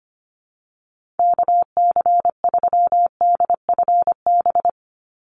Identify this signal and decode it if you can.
Morse code, heard as a keyed tone, about 25 wpm, 690 Hz. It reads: KC3DFB
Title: USA Call Signs 25wpm